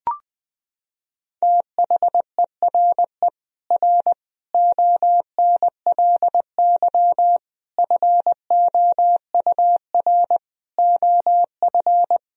Morse code, heard as a keyed tone, about 20 wpm, 700 Hz. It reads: THERE R ONLY FOUR OF
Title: Sentences from Top 300 Words and Abbreviations 20wpm